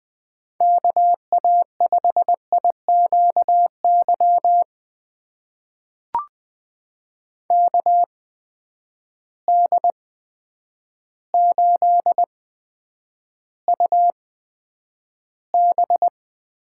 Morse code, heard as a keyed tone, about 20 wpm, spaced as for 9 wpm, 700 Hz. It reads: KA5IQY KD8UB
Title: USA Call Signs - 6 Characters - 8x Character Spacing - 20wpm